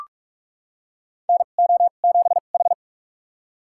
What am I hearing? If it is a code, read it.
Morse code, heard as a keyed tone, about 41 wpm, spaced as for 31 wpm, 700 Hz. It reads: NO7H
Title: USA Call Signs - 2x Character Spacing - 45wpm